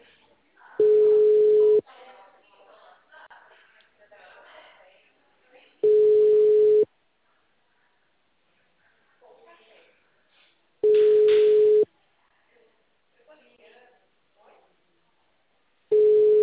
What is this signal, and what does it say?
Ringback tone (Europe, 425 Hz), with speech.
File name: out- -2005- - - .24870